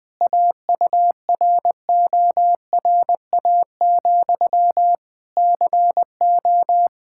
Morse code, heard as a keyed tone, about 21 wpm, 700 Hz. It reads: AURORA, CO
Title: K1USN SST QSO Practice 20wpm